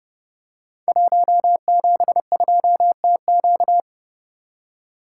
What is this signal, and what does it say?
Morse code, heard as a keyed tone, about 30 wpm, 700 Hz. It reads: 172TQ
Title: Code Groups - 5 Letter:Numbers - 30wpm